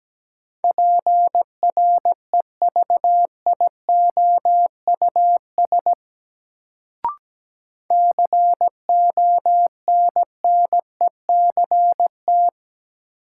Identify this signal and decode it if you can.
Morse code, heard as a keyed tone, about 17 wpm, 700 Hz. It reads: PREVIOUS CONNECT
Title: Sentences from Top 1500 Words - New Word Drill - 17wpm